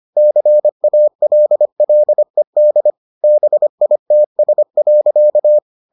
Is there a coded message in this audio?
Yes: Morse code from 0.2 to 5.6 s, heard as a keyed tone, about 25 wpm, 600 Hz. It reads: CALLED BITS.